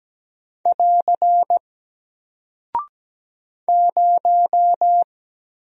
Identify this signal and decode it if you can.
Morse code, heard as a keyed tone, about 17 wpm, 700 Hz. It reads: <AR> 0
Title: Single Character (Letter:Number:Punctuation:Prosign) Alt 17wpm